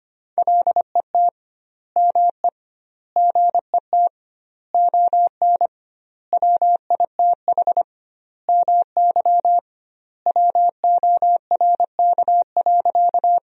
Morse code, heard as a keyed tone, about 25 wpm, 700 Hz. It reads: LET ME GET ON WITH MY WORK.
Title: 2x Word-Spacing - Sentences from Top 300 Words - 25wpm